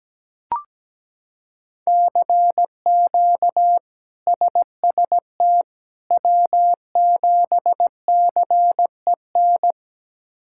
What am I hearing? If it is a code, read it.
Morse code, heard as a keyed tone, about 17 wpm, 700 Hz. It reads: CQ SST W7CEN
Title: K1USN SST QSO Practice - 17wpm